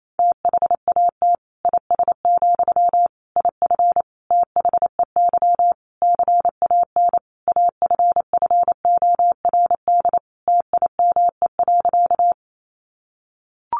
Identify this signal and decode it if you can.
Morse code, heard as a keyed tone, about 28 wpm, 700 Hz. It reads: THAT IS, IF THEY CAN AFFORD TIME.
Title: Sentences from Top 1100 Words 28wpm